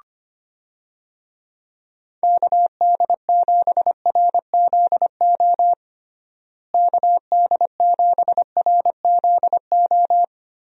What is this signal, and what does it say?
Morse code, heard as a keyed tone, about 25 wpm, 700 Hz. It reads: KD7RZO KD7RZO
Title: USA Call Signs - 6 Characters - Repeated 3x - 25wpm